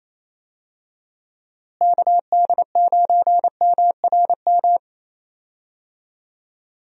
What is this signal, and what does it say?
Morse code, heard as a keyed tone, about 28 wpm, 700 Hz. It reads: KD9MRM
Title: USA Call Signs - 6 Characters - Repeated 2x - 28wpm